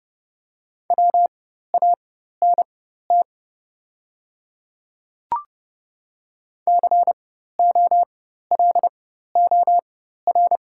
Morse code, heard as a keyed tone, about 30 wpm, spaced as for 14 wpm, 690 Hz. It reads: WANT COLOR
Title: 4x Character Spacing - Top 500 Words - 30wpm